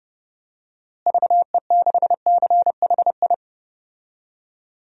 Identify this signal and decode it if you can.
Morse code, heard as a keyed tone, about 30 wpm, 700 Hz. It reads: VE6CHI
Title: DXCC-100 Call Signs - 6 Characters - Repeated 2x - 30wpm